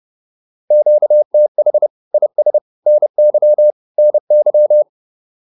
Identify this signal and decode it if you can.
Morse code, heard as a keyed tone, about 30 wpm, 600 Hz. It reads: QTH IS NY NY